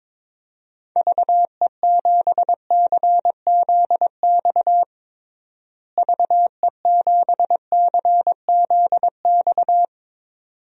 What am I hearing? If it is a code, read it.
Morse code, heard as a keyed tone, about 22 wpm, 700 Hz. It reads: VE7CZX VE7CZX
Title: DXCC-100 Call Signs - 6 Characters - Repeated 2x - 22wpm